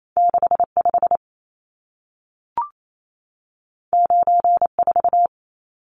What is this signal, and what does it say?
Morse code, heard as a keyed tone, about 28 wpm, 700 Hz. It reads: 65 94
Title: Numbers - 2-Digit 28wpm